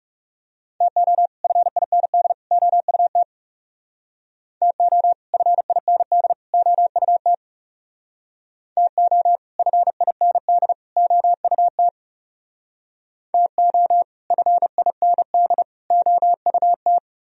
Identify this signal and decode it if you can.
Morse code, heard as a keyed tone, about 36 wpm, 700 Hz. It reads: TO FIND OUT TO FIND OUT TO FIND OUT TO FIND OUT
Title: Speed-Racing - Sets of 3 Words 30wpm